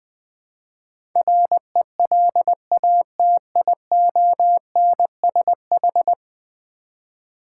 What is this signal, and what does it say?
Morse code, heard as a keyed tone, about 20 wpm, 700 Hz. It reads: RELATIONSH
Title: Build A Word - 5 Letters At A Time - 12-Letter Words - 20wpm